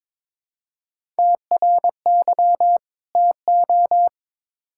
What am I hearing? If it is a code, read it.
Morse code, heard as a keyed tone, about 21 wpm, 700 Hz. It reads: TRY TO
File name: Build A Phrase - 2-Word Phrases - 22wpm